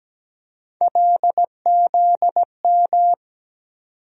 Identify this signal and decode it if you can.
Morse code, heard as a keyed tone, about 17 wpm, 700 Hz. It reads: LZM
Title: USA Call Sign Postfixes - 3 Letters 17wpm